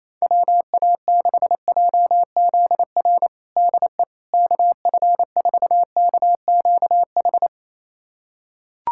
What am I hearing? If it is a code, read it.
Morse code, heard as a keyed tone, about 28 wpm, 700 Hz. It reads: WA6JZR DE KF4KQH
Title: USA Callsign DE Callsign - 6 Characters - 28wpm